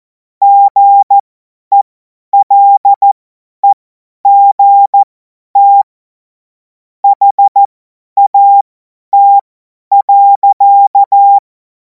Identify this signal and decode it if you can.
Morse code, heard as a keyed tone, about 14 wpm, spaced as for 10 wpm, 800 Hz. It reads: GELEGT HAT.
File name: Morsekurs / Witze / Witz 10. Februar 2026 50 F70